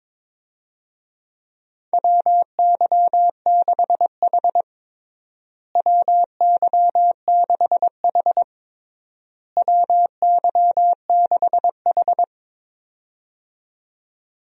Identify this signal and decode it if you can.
Morse code, heard as a keyed tone, about 22 wpm, 700 Hz. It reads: WY6H WY6H WY6H
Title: USA Call Signs - 4 Characters - Repeated 3x - 22wpm